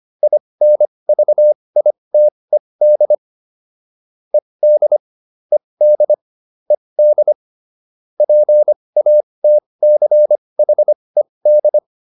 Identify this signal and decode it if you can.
Morse code, heard as a keyed tone, about 25 wpm, spaced as for 20 wpm, 600 Hz. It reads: INVITED ED ED ED PATCHED